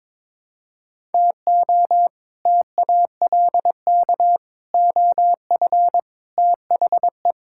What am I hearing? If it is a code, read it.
Morse code, heard as a keyed tone, about 22 wpm, 700 Hz. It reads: TO TALK OF THE
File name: practice / 22wpm / Build A Phrase - 5-Word Phrases - 22wpm